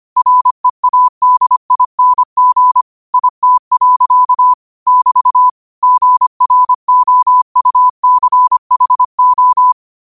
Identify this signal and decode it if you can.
Morse code, heard as a keyed tone, about 25 wpm, 1000 Hz. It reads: READING IT. <BT> GROUCHO